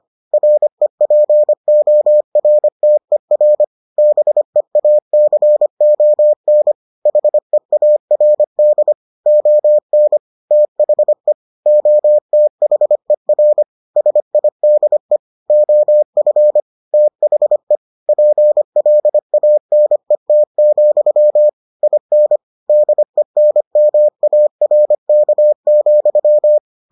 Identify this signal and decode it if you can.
Morse code, heard as a keyed tone, about 25 wpm, 600 Hz. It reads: REPORTER BEACON HEARD ON THE OTHER SIDE OF THE PLANET, IN DENMARK,